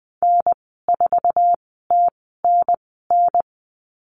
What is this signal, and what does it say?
Morse code, heard as a keyed tone, about 20 wpm, 700 Hz. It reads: N 4 T N N
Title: 048 Lesson 9g - Number 4 - Call Signs 20wpm